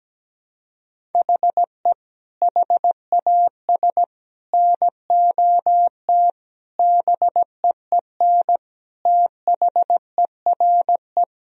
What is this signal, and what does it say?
Morse code, heard as a keyed tone, about 17 wpm, 700 Hz. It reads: HE HAS NOT BEEN THERE
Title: Sentences from Top 1000 Words - Review Part 1 - 17wpm